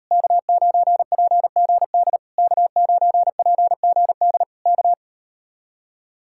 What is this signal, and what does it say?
Morse code, heard as a keyed tone, about 38 wpm, 700 Hz. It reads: K9PGD K9PGD K